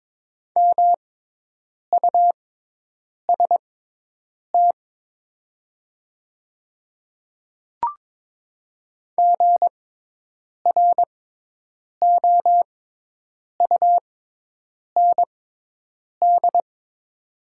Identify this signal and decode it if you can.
Morse code, heard as a keyed tone, about 22 wpm, spaced as for 8 wpm, 700 Hz. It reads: MUST GROUND
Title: 6x Character Spacing - Top 500 Words - 22wpm